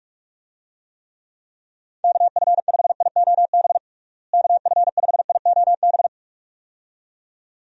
Morse code, heard as a keyed tone, about 46 wpm, 700 Hz. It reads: KF5IOB KF5IOB
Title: USA Call Signs - 6 Characters - Repeated 2x - 45wpm